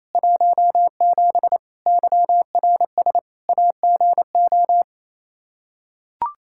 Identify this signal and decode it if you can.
Morse code, heard as a keyed tone, about 27 wpm, 700 Hz. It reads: 17 YRS AGO
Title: K1USN SST QSO Practice 28wpm